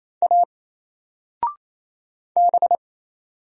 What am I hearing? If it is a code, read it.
Morse code, heard as a keyed tone, about 27 wpm, 700 Hz. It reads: A B
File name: Single-Letter-Number-28wpm